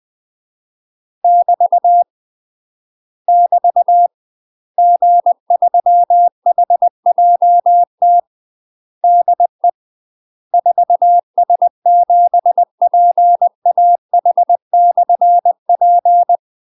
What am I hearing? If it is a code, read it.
Morse code, heard as a keyed tone, about 20 wpm, 700 Hz. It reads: <BT> <BT> G3HJT DE 4S7PAH/P
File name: UK MASTER QSOs TRAINING/20WPM